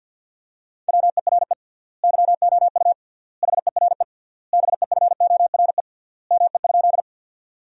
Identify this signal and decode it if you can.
Morse code, heard as a keyed tone, about 52 wpm, 700 Hz. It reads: WERE YOU HERE BEFORE ME?
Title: 3x Word-Spacing - Sentences from Top 200 Words - 50wpm